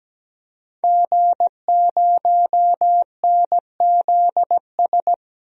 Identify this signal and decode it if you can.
Morse code, heard as a keyed tone, about 17 wpm, 700 Hz. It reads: G0NZS
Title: DXCC-100 Call Signs - 5 Characters - 17wpm